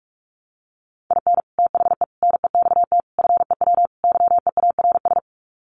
Morse code, heard as a keyed tone, about 44 wpm, 700 Hz. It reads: IN THE NEXT FEW YEARS
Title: Sets of 5 Words Encore 45wpm